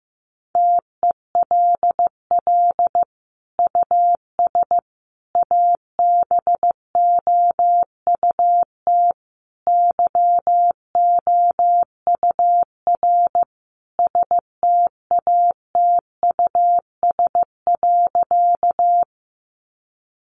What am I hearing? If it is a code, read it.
Morse code, heard as a keyed tone, about 15 wpm, 700 Hz. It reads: TELL US ABOUT YOUR STATUS.
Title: Sentences from Top 1400 Words 15wpm